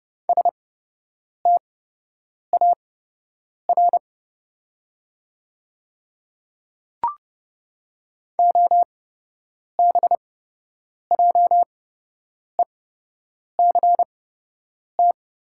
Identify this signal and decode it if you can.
Morse code, heard as a keyed tone, about 30 wpm, spaced as for 8 wpm, 700 Hz. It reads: STAR OBJECT
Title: 8x Character Spacing - Top 500 Words - 30wpm